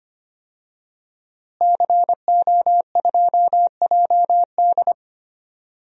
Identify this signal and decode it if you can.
Morse code, heard as a keyed tone, about 25 wpm, 700 Hz. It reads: CO2JD
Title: DXCC-100 Call Signs - 5 Characters - Repeated 3x - 25wpm